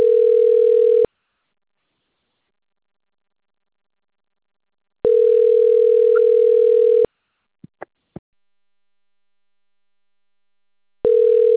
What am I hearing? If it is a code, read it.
Ringback tone (North America), 440 + 480 Hz.